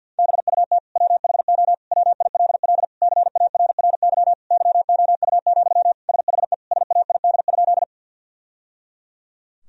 Morse code, heard as a keyed tone, about 53 wpm, 700 Hz. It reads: BUT WHO WILL CARRY YOU, SHE SAID?
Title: Sentences from Top 1000 Words - Review Part 1 - 50wpm